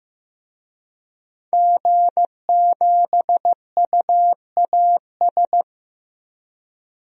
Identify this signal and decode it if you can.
Morse code, heard as a keyed tone, about 15 wpm, 700 Hz. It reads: G7UAS
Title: DXCC-100 Call Signs - 5 Characters - 15wpm